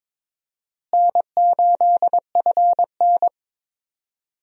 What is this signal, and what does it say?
Morse code, heard as a keyed tone, about 21 wpm, 700 Hz. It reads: N8FN
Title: USA Call Signs - 4 Characters - Repeated 2x - 22wpm